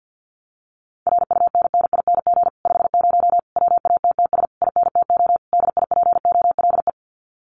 Morse code, heard as a keyed tone, about 48 wpm, 700 Hz. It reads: RUNNING 50 WATTS INTO DIPOLE
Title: Common QSO Phrases Encore 50wpm